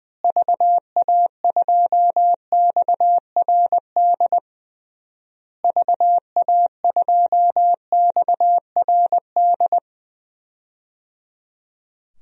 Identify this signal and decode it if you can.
Morse code, heard as a keyed tone, about 20 wpm, 700 Hz. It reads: VA2XRD VA2XRD
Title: DXCC-100 Call Signs - 6 Characters - Repeated 3x - 20wpm